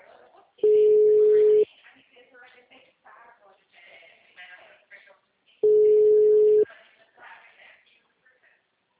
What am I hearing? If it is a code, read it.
Ringback tone (Europe, 425 Hz), with speech.
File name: out- -2010- - - .14165